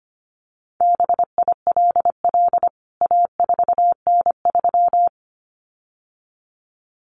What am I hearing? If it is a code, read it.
Morse code, heard as a keyed tone, about 25 wpm, 700 Hz. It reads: BILL A4N3
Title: CWT Exchange 25wpm